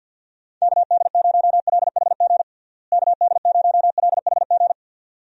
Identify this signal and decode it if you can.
Morse code, heard as a keyed tone, about 49 wpm, 700 Hz. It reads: KD0LHG KD0LHG
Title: USA Call Signs - 6 Characters - Repeated 2x - 50wpm